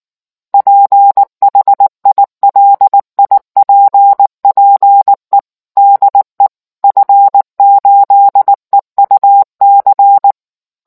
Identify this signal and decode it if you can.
Morse code, heard as a keyed tone, about 19 wpm, 800 Hz. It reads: PHILIPPE DE F8EUC